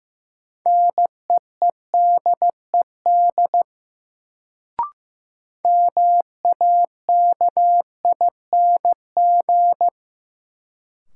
Morse code, heard as a keyed tone, about 15 wpm, 700 Hz. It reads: NEEDED MAKING
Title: Sentences from Top 1000 Words - Review Part 6 - New Word Drill - 15wpm